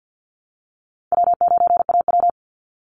Morse code, heard as a keyed tone, about 50 wpm, 700 Hz. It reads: W9AW
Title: Easy Call Signs Repeated 3x - 50wpm